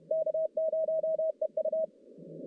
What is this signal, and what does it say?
Morse code, heard as a keyed tone, about 31 wpm, 640 Hz. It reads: K0EU
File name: contests / CW-SS-k5tr- -1200